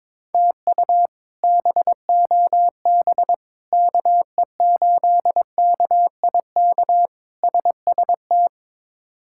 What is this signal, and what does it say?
Morse code, heard as a keyed tone, about 22 wpm, 700 Hz. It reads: TU BOB KE8KIK SST
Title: K1USN SST QSO Practice - 22wpm